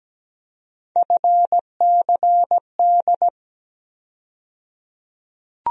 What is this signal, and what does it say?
Morse code, heard as a keyed tone, about 17 wpm, 700 Hz. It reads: FCD
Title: USA Call Sign Postfixes - 3 Letters 17wpm